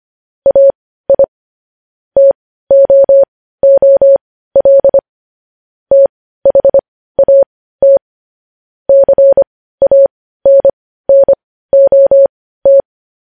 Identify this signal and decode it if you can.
Morse code, heard as a keyed tone, about 25 wpm, spaced as for 15 wpm, 550 Hz. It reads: AI TOOL THAT CANNOT
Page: news headlines in morse code at 15 wpm